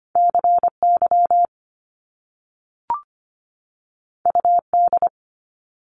Morse code, heard as a keyed tone, about 25 wpm, 700 Hz. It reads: C Y U D
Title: Top Two Letter Combinations in Words 25wpm